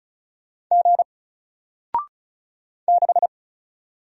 Morse code, heard as a keyed tone, about 36 wpm, 700 Hz. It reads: G 6
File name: Single-Letter-Number-35wpm